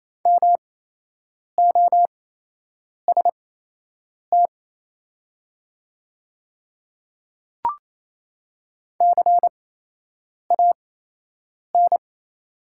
Morse code, heard as a keyed tone, about 28 wpm, spaced as for 8 wpm, 700 Hz. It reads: MOST CAN
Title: 8x Character Spacing - Top 300 Words - 28wpm